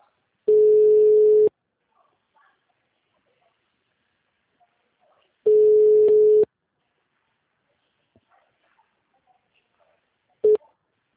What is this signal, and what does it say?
Ringback tone (Europe, 425 Hz), with speech.